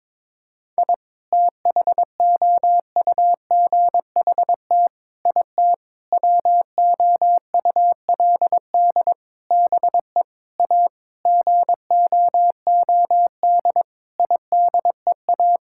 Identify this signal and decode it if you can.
Morse code, heard as a keyed tone, about 22 wpm, 700 Hz. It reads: I THOUGHT IT WOULD BE A GOOD IDEA
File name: Sets of 9 Words - 22wpm